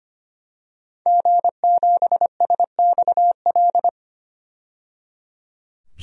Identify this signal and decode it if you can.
Morse code, heard as a keyed tone, about 25 wpm, 700 Hz. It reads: G7SXL
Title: DXCC-100 Call Signs - 5 Characters - 25wpm